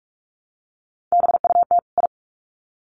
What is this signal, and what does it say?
Morse code, heard as a keyed tone, about 44 wpm, 700 Hz. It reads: BUT I
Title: Sets of 2 Words 45wpm